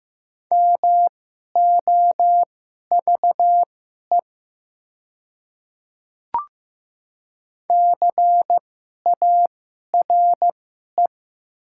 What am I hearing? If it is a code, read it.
Morse code, heard as a keyed tone, about 15 wpm, spaced as for 11 wpm, 700 Hz. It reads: MOVE CARE
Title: 2x Character Spacing - Top 300 Words - 15wpm